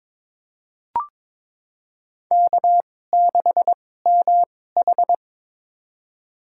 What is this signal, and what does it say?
Morse code, heard as a keyed tone, about 22 wpm, spaced as for 16 wpm, 700 Hz. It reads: K6MH
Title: USA Call Signs - 4 Characters - 2x Character Spacing - 22wpm